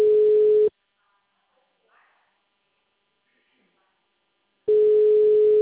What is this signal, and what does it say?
Ringback tone (Europe, 425 Hz).
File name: out- -2007- - - .68083